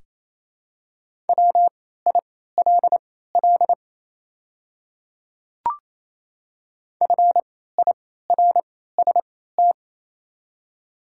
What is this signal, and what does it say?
Morse code, heard as a keyed tone, about 28 wpm, spaced as for 16 wpm, 700 Hz. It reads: WILL FIRST
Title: 3x Character Spacing - Top 100 Words - 28wpm